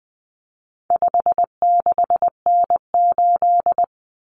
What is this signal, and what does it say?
Morse code, heard as a keyed tone, about 20 wpm, 700 Hz. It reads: 56N8
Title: Common Cut Numbers - 4-Digit 20wpm